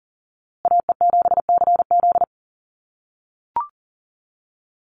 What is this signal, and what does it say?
Morse code, heard as a keyed tone, about 40 wpm, 700 Hz. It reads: AE7CZ
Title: USA Call Signs Repeated 3x 40wpm